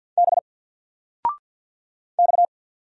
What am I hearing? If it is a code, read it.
Morse code, heard as a keyed tone, about 50 wpm, 700 Hz. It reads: B X
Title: Single Character (Letter:Number:Punctuation:Prosign) 50wpm